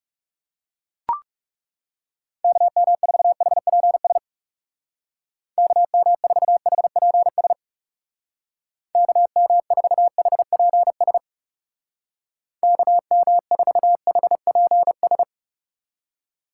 Morse code, heard as a keyed tone, about 39 wpm, 700 Hz. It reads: KM4HPS KM4HPS KM4HPS KM4HPS